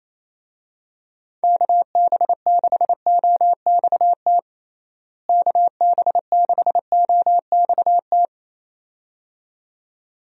Morse code, heard as a keyed tone, about 27 wpm, 700 Hz. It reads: KB6OXT KB6OXT
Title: USA Call Signs - 6 Characters - Repeated 2x - 28wpm